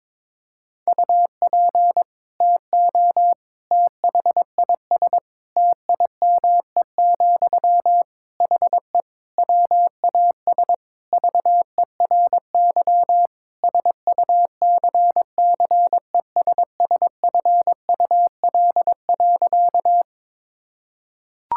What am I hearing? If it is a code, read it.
Morse code, heard as a keyed tone, about 22 wpm, 700 Hz. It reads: UP TO THIS TIME, HE WAS VERY SUCCESSFUL.
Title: Sentences from Top 1000 Words - Review Part 6 - 22wpm